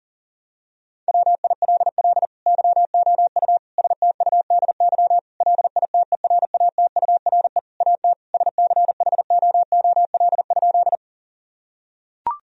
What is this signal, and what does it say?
Morse code, heard as a keyed tone, about 37 wpm, 700 Hz. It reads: WILL YOU STUDY LITERATURE AT SCHOOL?
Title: Sentences from Top 1000 Words - Review Part 3 - 40wpm